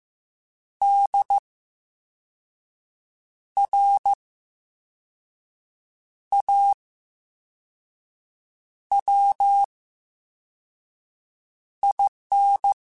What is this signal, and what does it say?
Morse code, heard as a keyed tone, about 15 wpm, 800 Hz. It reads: D R A W I N
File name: Special INGs-mp3s / Special INGs-mp3s / VerySlowToFaster Ing @13